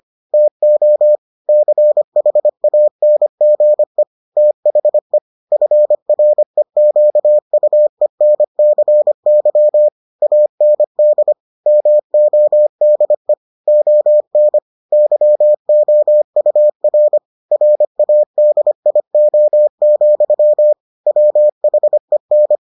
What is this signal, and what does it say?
Morse code, heard as a keyed tone, about 25 wpm, 600 Hz. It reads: TO CHANGE THE FREQUENCY AND MODE ON YOUR RADIO, WHEN